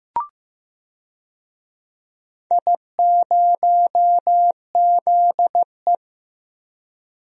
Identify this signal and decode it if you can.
Morse code, heard as a keyed tone, about 15 wpm, 700 Hz. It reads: I0ZE
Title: DXCC-100 Call Signs - 4 Characters - 15wpm